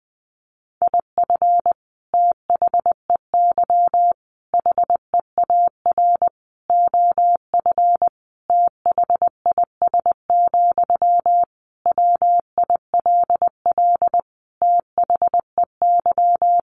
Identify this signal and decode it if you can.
Morse code, heard as a keyed tone, about 19 wpm, 700 Hz. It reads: IF THEY HEAR OF THIS, WILL THEY
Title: Sentences from Top 400 Words 20wpm